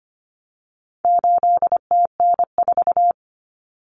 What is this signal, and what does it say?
Morse code, heard as a keyed tone, about 25 wpm, 700 Hz. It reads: 8TN4
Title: Common Cut Numbers - 4-Digit 25wpm